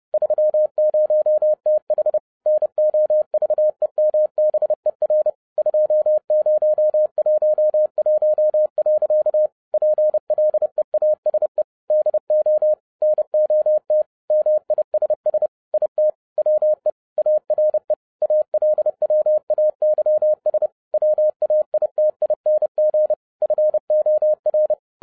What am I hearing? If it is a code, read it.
Morse code, heard as a keyed tone, about 30 wpm, 610 Hz. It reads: 30TH NOVEMBER 2011. PLEASE DO NOT MISS IT WE ARE ALWAYS WAITING FOR